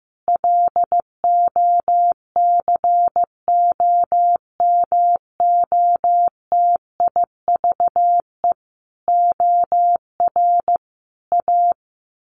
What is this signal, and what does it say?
Morse code, heard as a keyed tone, about 15 wpm, 700 Hz. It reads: LOCOMOTIVE OR A
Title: A Sound Of Thunder 15wpm